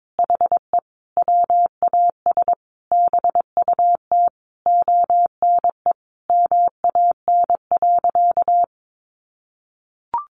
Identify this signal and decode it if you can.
Morse code, heard as a keyed tone, about 22 wpm, 700 Hz. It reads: HE WAS BUT ONE MAN.
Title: Sentences from Top 1500 Words 22wpm